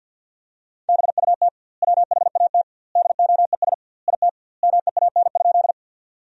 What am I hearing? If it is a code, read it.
Morse code, heard as a keyed tone, about 50 wpm, 700 Hz. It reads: BUT WHAT DOES IT MEAN?
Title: 2x Word-Spacing - Sentences from Top 200 Words - 50wpm